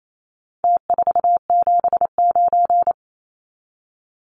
Morse code, heard as a keyed tone, about 28 wpm, 700 Hz. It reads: T479
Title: Numbers - Contest Style - 4-Digit 28wpm